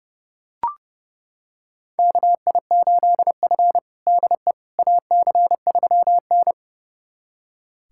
Morse code, heard as a keyed tone, about 32 wpm, 700 Hz. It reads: KI8F DE AC3N
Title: USA Callsign DE Callsign - 4 Characters - 30wpm